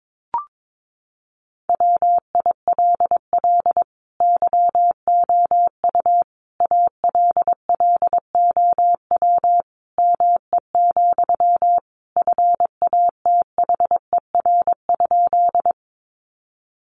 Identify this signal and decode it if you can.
Morse code, heard as a keyed tone, about 22 wpm, 700 Hz. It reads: WILL YOU ALLOW ME, FATHER?
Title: Sentences from Top 700 Words 22wpm